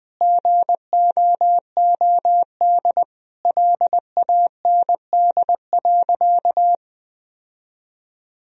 Morse code, heard as a keyed tone, about 20 wpm, 700 Hz. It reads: GOOD LAND.
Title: Sentences from Top 1000 Words - Review Part 1 - 20wpm